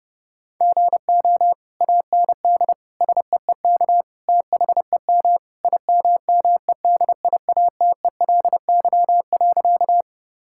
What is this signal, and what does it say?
Morse code, heard as a keyed tone, about 28 wpm, 700 Hz. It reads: GO AND SEEK THEM IMMEDIATELY.
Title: Sentences from Top 1600 Words 30wpm